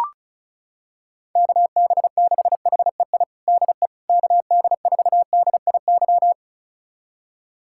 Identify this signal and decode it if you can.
Morse code, heard as a keyed tone, about 36 wpm, 700 Hz. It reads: KB6HEI DE KD4DIY
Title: USA Callsign DE Callsign - 6 Characters - 35wpm